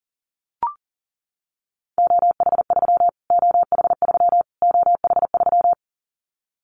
Morse code, heard as a keyed tone, about 40 wpm, 700 Hz. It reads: OH3 OH3 OH3
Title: Callsign Prefixes for Top 25 DXCC - Repeated 3x - 40wpm